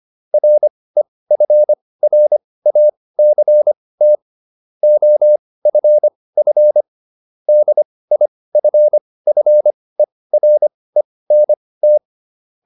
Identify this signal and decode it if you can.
Morse code, heard as a keyed tone, about 25 wpm, spaced as for 18 wpm, 600 Hz. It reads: REFRACT OFF DIFFERENT
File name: SS211 18